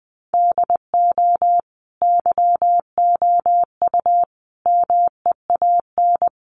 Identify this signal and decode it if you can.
Morse code, heard as a keyed tone, about 20 wpm, 700 Hz. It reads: DO YOU MEAN
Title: Sets of 5 Words Encore2 20wpm